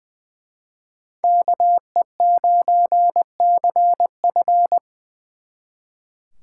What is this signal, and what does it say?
Morse code, heard as a keyed tone, about 24 wpm, 700 Hz. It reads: KE9CF
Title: Speed Racing - USA Call Signs 20wpm